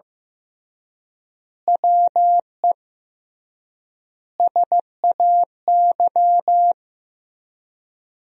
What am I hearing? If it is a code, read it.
Morse code, heard as a keyed tone, about 15 wpm, 700 Hz. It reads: WE SAY
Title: 3x Word-Spacing - Sentences from Top 200 Words - 15wpm